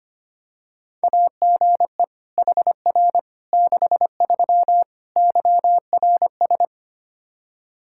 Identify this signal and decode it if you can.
Morse code, heard as a keyed tone, about 25 wpm, 700 Hz. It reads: AGE HR 63 YRS
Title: Practice QSO - Exchange 3 - More Info - 25wpm